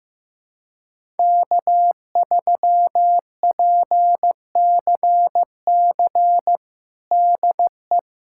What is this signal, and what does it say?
Morse code, heard as a keyed tone, about 16 wpm, 700 Hz. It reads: K3PCC DE
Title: USA Callsign DE Callsign - 5 Characters - 15wpm